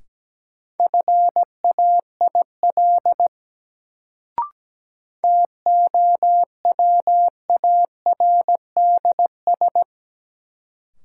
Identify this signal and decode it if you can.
Morse code, heard as a keyed tone, about 17 wpm, 700 Hz. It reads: FAIL TOWARDS
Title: Sentences from Top 1100 Words - New Word Drill Alt - 17wpm